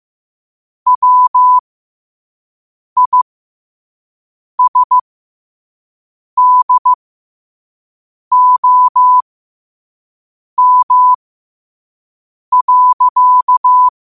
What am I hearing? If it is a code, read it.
Morse code, heard as a keyed tone, about 15 wpm, spaced as for 5 wpm, 1000 Hz. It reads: WISDOM.
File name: qotdfiles / 5wpm